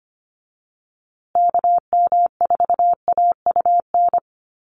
Morse code, heard as a keyed tone, about 25 wpm, 700 Hz. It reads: KM4AUN
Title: USA Call Signs Repeated 2x 25wpm